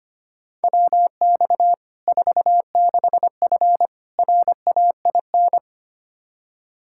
Morse code, heard as a keyed tone, about 25 wpm, 700 Hz. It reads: WX 46F RAIN
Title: Practice QSO - Exchange 2 - Info - 25wpm